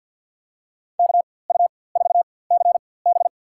Morse code, heard as a keyed tone, about 49 wpm, spaced as for 23 wpm, 700 Hz. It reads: KU4CB
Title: USA Call Signs - 5 Characters - 4x Character Spacing - 50wpm